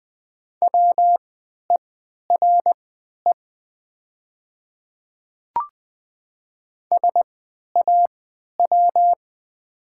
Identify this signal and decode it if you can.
Morse code, heard as a keyed tone, about 20 wpm, spaced as for 11 wpm, 690 Hz. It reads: WERE SAW
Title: 3x Character Spacing - Top 300 Words - 20wpm